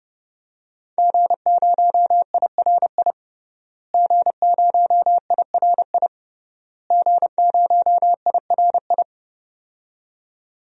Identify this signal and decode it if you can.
Morse code, heard as a keyed tone, about 30 wpm, 700 Hz. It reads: G0IRI G0IRI G0IRI
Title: DXCC-100 Call Signs - 5 Characters - Repeated 3x - 30wpm